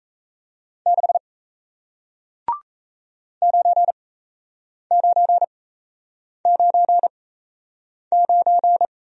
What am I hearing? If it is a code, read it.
Morse code, heard as a keyed tone, about 36 wpm, spaced as for 8 wpm, 700 Hz. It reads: 6 9999